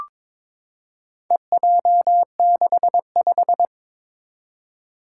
Morse code, heard as a keyed tone, about 22 wpm, 700 Hz. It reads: EJ65
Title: Four Challenging Letter-Number Combinations 22wpm